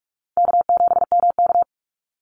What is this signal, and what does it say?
Morse code, heard as a keyed tone, about 44 wpm, 700 Hz. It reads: K7MK
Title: NA SOTA Activator Callsigns 45wpm